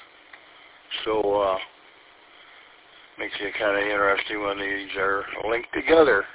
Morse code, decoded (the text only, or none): none